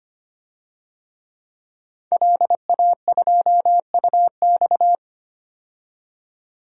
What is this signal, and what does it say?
Morse code, heard as a keyed tone, about 25 wpm, 700 Hz. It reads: LA2UX
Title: DXCC-100 Call Signs - 5 Characters - 25wpm